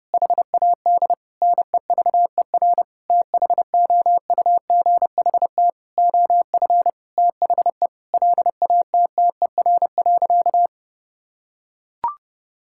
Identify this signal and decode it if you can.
Morse code, heard as a keyed tone, about 30 wpm, 700 Hz. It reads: HAD NEVER THOUGHT OF THE LATTER.
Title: Sentences from Top 1600 Words 30wpm